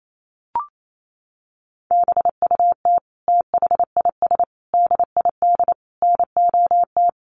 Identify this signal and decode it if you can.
Morse code, heard as a keyed tone, about 28 wpm, 700 Hz. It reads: BUT THIS DID NOT
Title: Sentences from Top 1200 Words 28wpm